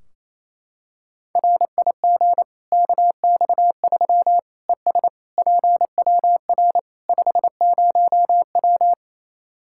Morse code, heard as a keyed tone, about 28 wpm, 700 Hz. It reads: RIG KX3 ES PWR 50W
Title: Practice QSO - Exchange 2 - Info - 28wpm